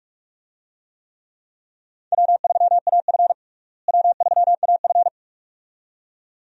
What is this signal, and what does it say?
Morse code, heard as a keyed tone, about 41 wpm, 700 Hz. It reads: W3AF W3AF
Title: USA Call Signs - 4 Characters - Repeated 2x - 45wpm